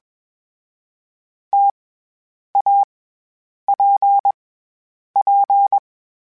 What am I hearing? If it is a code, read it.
Morse code, heard as a keyed tone, about 22 wpm, 800 Hz. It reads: T A P P
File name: Variable Spacing / Variable Spacing-mp3s / T AOEs @20